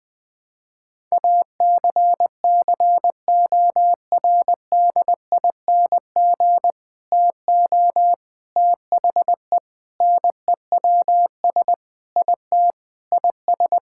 Morse code, heard as a keyed tone, about 20 wpm, 700 Hz. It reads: ACCORDING TO THE NEWS IT IS
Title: Sentences with 3-Word Phrases - 20wpm